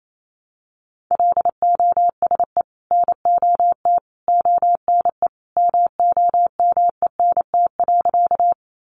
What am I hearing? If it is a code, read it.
Morse code, heard as a keyed tone, about 28 wpm, 700 Hz. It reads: LOSE NOT ONE MOMENT.
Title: Sentences from Top 1100 Words 28wpm